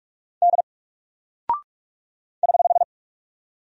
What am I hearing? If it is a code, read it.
Morse code, heard as a keyed tone, about 44 wpm, 700 Hz. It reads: D <HH>
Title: Single Character (Letter:Number:Punctuation:Prosign) 45wpm